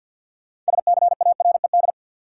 Morse code, heard as a keyed tone, about 50 wpm, 700 Hz. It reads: SCARED
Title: Sentences from Top 700 Words - New Word Drill - 50wpm